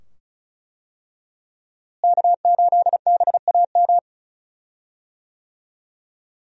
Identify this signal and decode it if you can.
Morse code, heard as a keyed tone, about 34 wpm, 700 Hz. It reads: K8BAM
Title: USA Call Signs - 5 Characters - Repeated 2x - 35wpm